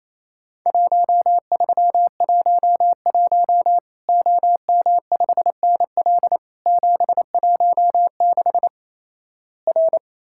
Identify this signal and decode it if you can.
Morse code, heard as a keyed tone, about 28 wpm, 700 Hz. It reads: 1311 OM5NL 716 R
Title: Worked All Europe DX Contest - QTC Exchange - 28wpm